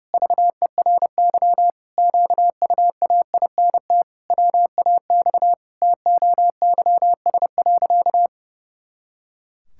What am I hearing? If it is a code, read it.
Morse code, heard as a keyed tone, about 32 wpm, 700 Hz. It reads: VERY QUAINT WAX TOYS.
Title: Pangram Challenge v2 - 30wpm